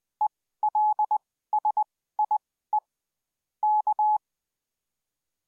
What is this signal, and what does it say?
Morse code, heard as a keyed tone, about 20 wpm, spaced as for 14 wpm, 850 Hz. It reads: ELSIE K